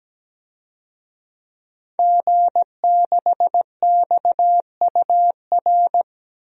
Morse code, heard as a keyed tone, about 17 wpm, 700 Hz. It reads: G6XUR
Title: DXCC-100 Call Signs - 5 Characters - 17wpm